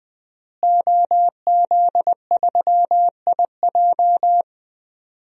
Morse code, heard as a keyed tone, about 21 wpm, 700 Hz. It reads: OZ3IJ
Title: DXCC-100 Call Signs - 5 Characters - Repeated 3x - 20wpm